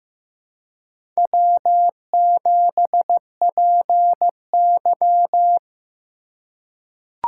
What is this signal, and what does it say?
Morse code, heard as a keyed tone, about 15 wpm, 700 Hz. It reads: W7PY
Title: USA Call Signs - 4 Characters - 2x Character Spacing - 15wpm